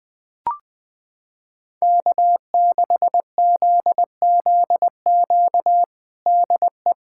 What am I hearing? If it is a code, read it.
Morse code, heard as a keyed tone, about 20 wpm, 700 Hz. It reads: K6ZZQ DE
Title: USA Callsign DE Callsign 20wpm